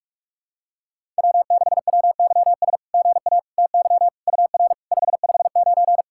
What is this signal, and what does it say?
Morse code, heard as a keyed tone, about 44 wpm, 690 Hz. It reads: W6WYS GA TY UR 559
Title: SOTA QSO Practice 45wpm